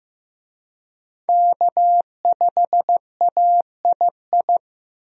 Morse code, heard as a keyed tone, about 15 wpm, 700 Hz. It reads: K5AII
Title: USA Call Signs - 5 Characters - Repeated 2x - 15wpm